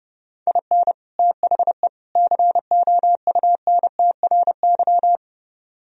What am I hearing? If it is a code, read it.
Morse code, heard as a keyed tone, about 29 wpm, 700 Hz. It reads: IN THE COUNTRY